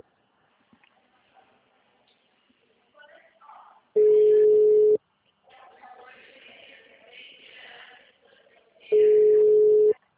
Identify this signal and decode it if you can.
Ringback tone (Europe, 425 Hz), with speech.